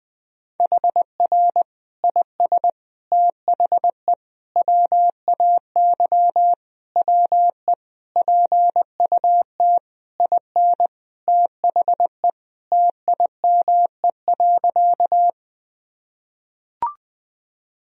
Morse code, heard as a keyed tone, about 20 wpm, 700 Hz. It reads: HR IS THE WAY WE PUT IN THE TIME.
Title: Sentences from Top 200 Words and Abbreviations 20wpm